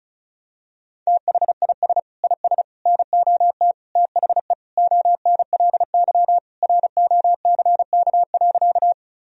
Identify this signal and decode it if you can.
Morse code, heard as a keyed tone, about 36 wpm, 700 Hz. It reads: THIS IS NOT THE ONLY ROCK.
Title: Sentences from Top 1000 Words - Review Part 2 - 35wpm